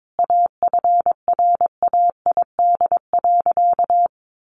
Morse code, heard as a keyed tone, about 22 wpm, 700 Hz. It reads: AFRAID.
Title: Sentences from Top 1500 Words 22wpm